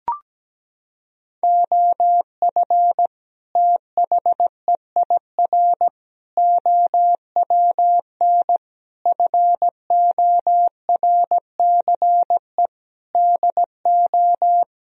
Morse code, heard as a keyed tone, about 17 wpm, 700 Hz. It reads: OF THEIR OWN FORCE DO
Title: Sentences from Top 1000 Words - Review Part 3 - 17wpm